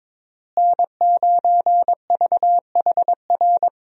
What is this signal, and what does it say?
Morse code, heard as a keyed tone, about 22 wpm, 700 Hz. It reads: N9VHR
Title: USA Call Signs - 5 Characters - Repeated 3x - 22wpm